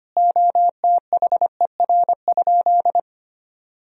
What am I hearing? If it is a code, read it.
Morse code, heard as a keyed tone, about 25 wpm, 700 Hz. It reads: OTHER?
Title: Sentences from Top 1000 Words - Review Part 1 - 25wpm